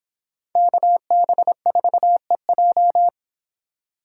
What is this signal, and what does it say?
Morse code, heard as a keyed tone, about 31 wpm, 700 Hz. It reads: KB4EJ